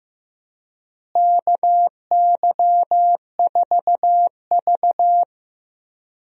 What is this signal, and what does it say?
Morse code, heard as a keyed tone, about 15 wpm, 700 Hz. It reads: KY4V
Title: USA Call Signs - 4 Characters - Repeated 2x - 15wpm